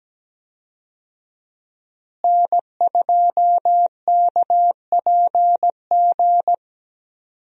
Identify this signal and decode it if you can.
Morse code, heard as a keyed tone, about 17 wpm, 700 Hz. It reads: N2KPG
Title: USA Call Signs - 5 Characters - Repeated 2x - 17wpm